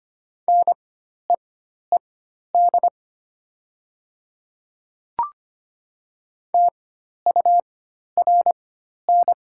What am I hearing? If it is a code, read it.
Morse code, heard as a keyed tone, about 25 wpm, spaced as for 12 wpm, 700 Hz. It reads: NEED TURN
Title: 4x Character Spacing - Top 300 Words - 25wpm